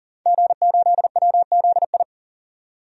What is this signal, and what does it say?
Morse code, heard as a keyed tone, about 39 wpm, 700 Hz. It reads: G8WZI
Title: DXCC-100 Call Signs - 5 Characters - 40wpm